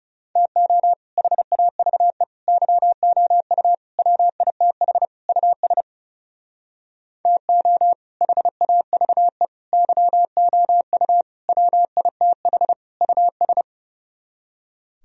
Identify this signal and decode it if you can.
Morse code, heard as a keyed tone, about 40 wpm, 700 Hz. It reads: TO HAVE YOU WITH US TO HAVE YOU WITH US